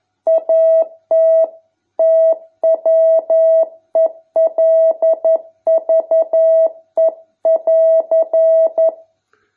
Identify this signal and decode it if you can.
Morse code, heard as a keyed tone, about 11 wpm, 650 Hz. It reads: AT TWELVE<AR>